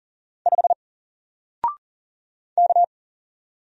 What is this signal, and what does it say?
Morse code, heard as a keyed tone, about 40 wpm, 700 Hz. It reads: 5 K
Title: Single Letter-Number Alt 40wpm